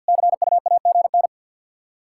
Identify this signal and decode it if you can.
Morse code, heard as a keyed tone, about 50 wpm, 700 Hz. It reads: CUAGN
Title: Practice QSO - Exchange 4 - Ending - 50wpm